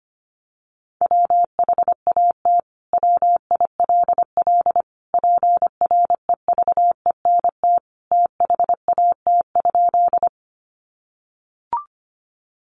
Morse code, heard as a keyed tone, about 25 wpm, 700 Hz. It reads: WHAT WILL PREVENT THAT?
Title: Sentences from Top 1100 Words 25wpm